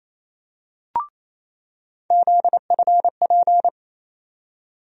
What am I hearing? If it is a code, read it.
Morse code, heard as a keyed tone, about 28 wpm, 700 Hz. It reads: ZFP
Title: USA Call Sign Postfixes - 3 Letters 28wpm